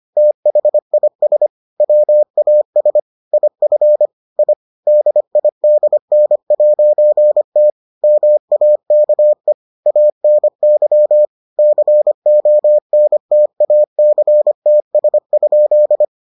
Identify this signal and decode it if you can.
Morse code, heard as a keyed tone, about 25 wpm, 600 Hz. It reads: THIS WAS IF I DIDN'T MAKE ANY CONTACTS?